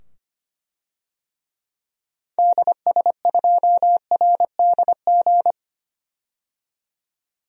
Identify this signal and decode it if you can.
Morse code, heard as a keyed tone, about 25 wpm, 700 Hz. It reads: DS2RDG
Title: DXCC-100 Call Signs - 6 Characters - 25wpm